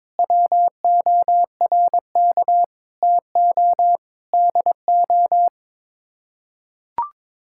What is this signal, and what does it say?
Morse code, heard as a keyed tone, about 22 wpm, 700 Hz. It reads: WORK TO DO
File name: Sets of 9 Words - 22wpm